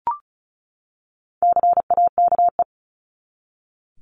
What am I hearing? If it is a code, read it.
Morse code, heard as a keyed tone, about 34 wpm, 700 Hz. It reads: CAKE
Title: Top 1501-1750 Words 35wpm